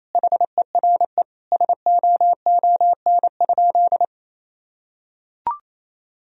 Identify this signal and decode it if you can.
Morse code, heard as a keyed tone, about 28 wpm, 700 Hz. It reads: HERE SOON?
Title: Sentences from Top 1000 Words - Review Part 3 - 28wpm